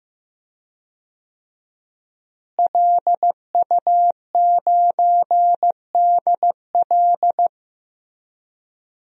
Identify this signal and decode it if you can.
Morse code, heard as a keyed tone, about 15 wpm, 700 Hz. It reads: LU9DL
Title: DXCC-100 Call Signs - 5 Characters - Repeated 2x - 15wpm